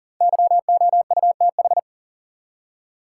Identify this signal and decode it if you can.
Morse code, heard as a keyed tone, about 40 wpm, 700 Hz. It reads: YOUTH
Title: Five Letter Words - 40wpm